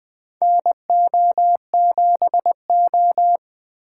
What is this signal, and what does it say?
Morse code, heard as a keyed tone, about 21 wpm, 700 Hz. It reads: NO7O
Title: USA Call Signs - 4 Characters - 2x Character Spacing - 20wpm